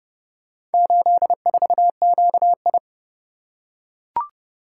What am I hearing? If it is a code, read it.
Morse code, heard as a keyed tone, about 30 wpm, 700 Hz. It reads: 84QI